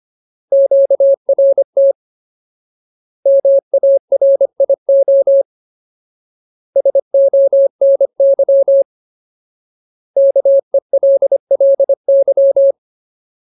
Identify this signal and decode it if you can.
Morse code, heard as a keyed tone, about 25 wpm, 550 Hz. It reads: QRT MARIO SONY KELLY
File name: Ultimate Word List / 25WPM Ultimate